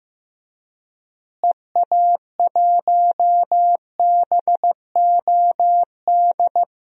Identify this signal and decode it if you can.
Morse code, heard as a keyed tone, about 15 wpm, 700 Hz. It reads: EA1BOD
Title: DXCC-100 Call Signs - 6 Characters - Repeated 2x - 15wpm